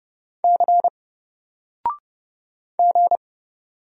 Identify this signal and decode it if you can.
Morse code, heard as a keyed tone, about 31 wpm, 700 Hz. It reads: C G
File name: Single-Letter-Number-30wpm